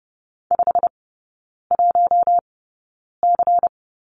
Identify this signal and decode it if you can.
Morse code, heard as a keyed tone, about 30 wpm, spaced as for 9 wpm, 690 Hz. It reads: 51C
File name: Lesson 17e - Letter u - Three Characters 30wpm